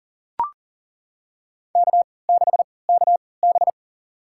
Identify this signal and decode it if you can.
Morse code, heard as a keyed tone, about 40 wpm, spaced as for 23 wpm, 700 Hz. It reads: K6KB
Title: USA Call Signs - 4 Characters - 3x Character Spacing - 40wpm